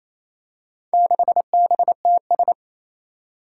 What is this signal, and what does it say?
Morse code, heard as a keyed tone, about 28 wpm, 700 Hz. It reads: 6BTS
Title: Four Challenging Letter-Number Combinations 28wpm